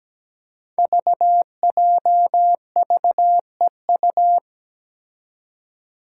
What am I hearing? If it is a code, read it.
Morse code, heard as a keyed tone, about 17 wpm, 700 Hz. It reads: VJVEU
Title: Five Challenging Letter-Number Combinations 17wpm